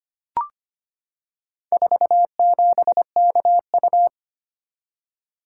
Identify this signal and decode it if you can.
Morse code, heard as a keyed tone, about 25 wpm, 690 Hz. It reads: 47KU
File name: practice / 25wpm / Four Letter-Number Combinations - 25wpm